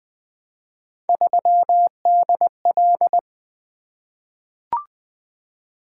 Morse code, heard as a keyed tone, about 20 wpm, 700 Hz. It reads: 3DL
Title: Three Letter-Number Combinations - 20wpm